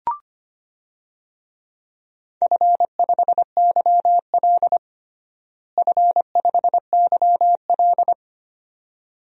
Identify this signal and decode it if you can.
Morse code, heard as a keyed tone, about 25 wpm, 700 Hz. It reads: F5YL F5YL
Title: DXCC-100 Call Signs - 4 Characters - Repeated 2x - 25wpm